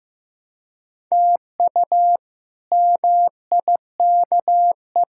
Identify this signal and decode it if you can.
Morse code, heard as a keyed tone, about 15 wpm, 700 Hz. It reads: TU MIKE
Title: K1USN SST QSO Practice - 15wpm